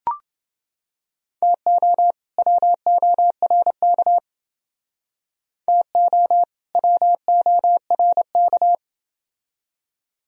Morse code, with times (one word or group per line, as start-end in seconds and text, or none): 1.4-2.1 TO
2.4-4.2 WORK
5.7-6.4 TO
6.8-8.7 WORK